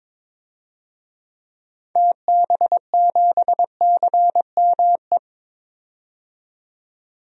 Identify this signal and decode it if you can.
Morse code, heard as a keyed tone, about 23 wpm, 700 Hz. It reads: TB7CME
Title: DXCC-100 Call Signs - 6 Characters - 22wpm